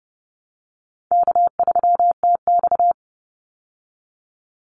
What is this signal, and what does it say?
Morse code, heard as a keyed tone, about 30 wpm, 700 Hz. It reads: K3TX
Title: Easy Call Signs Repeated 3x - 30wpm